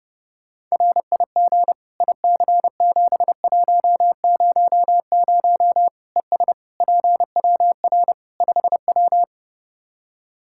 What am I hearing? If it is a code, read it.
Morse code, heard as a keyed tone, about 31 wpm, 700 Hz. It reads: RIG IC7100 ES PWR 5W
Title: Practice QSO - Exchange 2 - Info - 30wpm